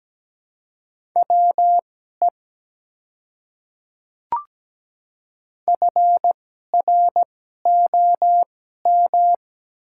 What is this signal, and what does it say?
Morse code, heard as a keyed tone, about 17 wpm, spaced as for 12 wpm, 690 Hz. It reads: WE FROM
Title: 2x Character Spacing - Top 100 Words - 17wpm